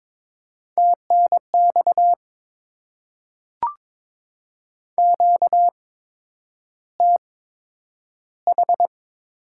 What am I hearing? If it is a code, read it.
Morse code, heard as a keyed tone, about 22 wpm, spaced as for 9 wpm, 700 Hz. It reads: TNX QTH
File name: Top 20 QSO Elements - 8x Character Spacing - 22wpm